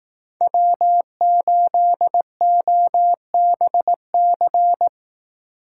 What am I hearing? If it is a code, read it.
Morse code, heard as a keyed tone, about 19 wpm, 700 Hz. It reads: W8OBC